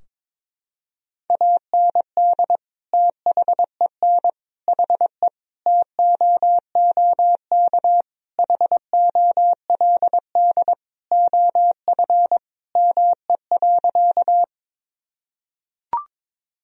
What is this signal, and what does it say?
Morse code, heard as a keyed tone, about 22 wpm, 700 Hz. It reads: AND THEN HE TOOK HOLD OF ME.
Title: Sentences from Top 1000 Words - Review Part 2 - 22wpm